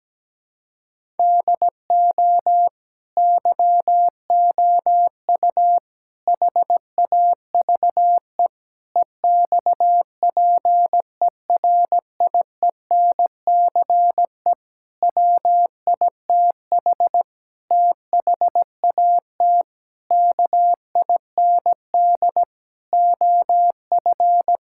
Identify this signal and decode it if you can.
Morse code, heard as a keyed tone, about 17 wpm, 700 Hz. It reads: DO YOU HAVE EXPERIENCE WITH THAT KIND OF
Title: Sentences from Top 1000 Words - Review Part 5 - 17wpm